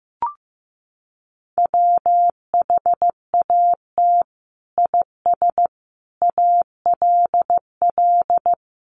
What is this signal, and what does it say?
Morse code, heard as a keyed tone, about 15 wpm, 700 Hz. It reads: WHAT IS ALL
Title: Sentences from Top 1200 Words 15wpm